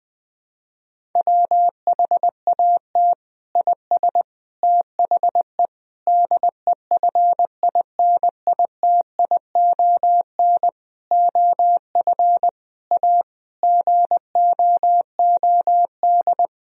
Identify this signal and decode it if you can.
Morse code, heard as a keyed tone, about 20 wpm, 700 Hz. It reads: WHAT IS THE DEFINITION OF A GOOD
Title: Sentences from Top 1000 Words - Review Part 4 - 20wpm